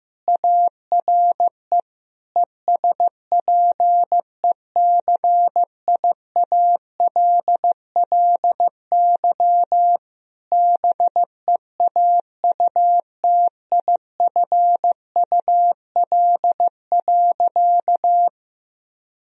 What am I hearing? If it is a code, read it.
Morse code, heard as a keyed tone, about 14 wpm, 700 Hz. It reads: ARE ESPECIALLY BEAUTIFUL.
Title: Sentences from Top 1000 Words - Review Part 4 - 15wpm